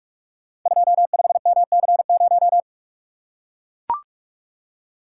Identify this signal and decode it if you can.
Morse code, heard as a keyed tone, about 44 wpm, 700 Hz. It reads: 25MC0
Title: Code Groups - 5 Letter:Numbers - 45wpm